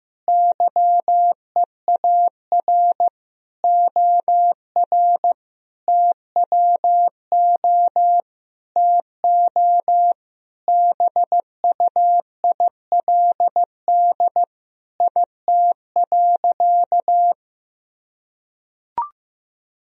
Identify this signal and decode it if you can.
Morse code, heard as a keyed tone, about 15 wpm, 700 Hz. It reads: YEAR OR TWO TO BUILD IT.
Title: Sentences from Top 1000 Words - Review Part 1 - 15wpm